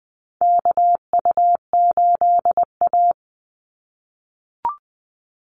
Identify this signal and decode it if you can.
Morse code, heard as a keyed tone, about 20 wpm, 700 Hz. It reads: KU8A
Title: USA Call Signs 20wpm